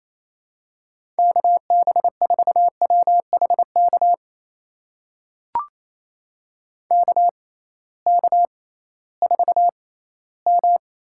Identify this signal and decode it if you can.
Morse code, heard as a keyed tone, about 28 wpm, spaced as for 16 wpm, 700 Hz. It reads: KB4WHK KK4M
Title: USA Call Signs - 6 Characters - 6x Character Spacing - 28wpm